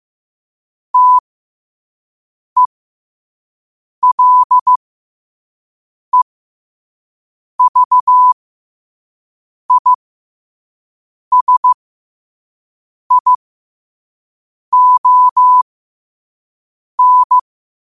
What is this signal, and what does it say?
Morse code, heard as a keyed tone, about 15 wpm, spaced as for 5 wpm, 1000 Hz. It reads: TELEVISION
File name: qotdfiles / 5wpm